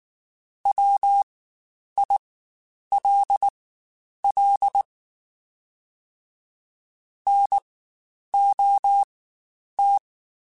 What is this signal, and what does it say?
Morse code, heard as a keyed tone, about 19 wpm, spaced as for 9 wpm, 800 Hz. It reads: WILL NOT
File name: Quotes Spaced / Quotes Spaced-mp3s / IgnoringFacts sp @18